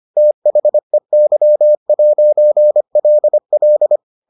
Morse code, heard as a keyed tone, about 25 wpm, 600 Hz. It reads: THEY'LL